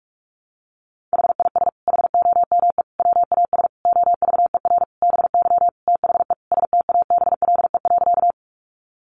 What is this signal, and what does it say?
Morse code, heard as a keyed tone, about 44 wpm, 700 Hz. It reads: HIS HOME WAS OVER BY THE STABLE.
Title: Sentences from Top 1500 Words 45wpm